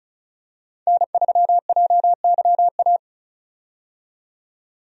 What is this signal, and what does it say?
Morse code, heard as a keyed tone, about 34 wpm, 700 Hz. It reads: N3JYA
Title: USA Call Signs - 5 Characters - 35wpm